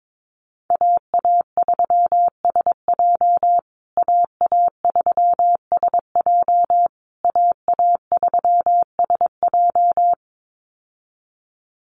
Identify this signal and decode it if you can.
Morse code, heard as a keyed tone, about 22 wpm, 700 Hz. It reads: AA3SJ AA3SJ AA3SJ
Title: USA Call Signs Repeated 3x 22wpm